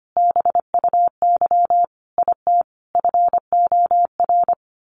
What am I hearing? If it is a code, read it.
Morse code, heard as a keyed tone, about 25 wpm, 700 Hz. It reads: BUY IT FOR
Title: Sentences from Top 1400 Words 25wpm